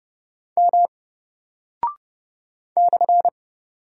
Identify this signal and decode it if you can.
Morse code, heard as a keyed tone, about 30 wpm, 700 Hz. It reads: M /
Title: Single Character (Letter:Number:Punctuation:Prosign) Alt 30wpm